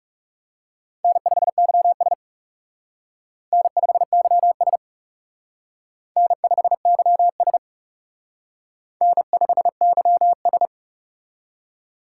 Morse code, heard as a keyed tone, about 37 wpm, 700 Hz. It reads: N5YS N5YS N5YS N5YS
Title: Speed Racing - USA Call Signs 30wpm